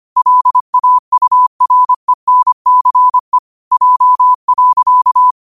Morse code, heard as a keyed tone, about 25 wpm, 1000 Hz. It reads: LAURENCE J.